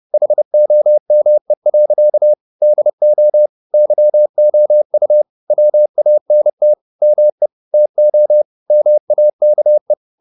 Morse code, heard as a keyed tone, about 30 wpm, 600 Hz. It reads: HOME. DO YOU WANT ME TO MAKE
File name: the old man and the sea / the old man and the sea 1 30wpm